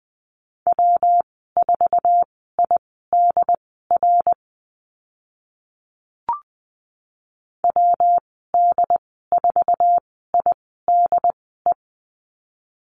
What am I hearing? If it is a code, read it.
Morse code, heard as a keyed tone, about 20 wpm, spaced as for 15 wpm, 700 Hz. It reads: W4IDR WD4IDE
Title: 154 Lesson 25g - Letter p - Call Signs 20wpm